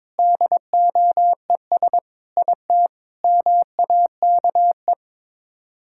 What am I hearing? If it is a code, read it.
Morse code, heard as a keyed tone, about 22 wpm, 700 Hz. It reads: DOES IT MAKE
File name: practice / 22wpm / Sets of 8 Words - 22wpm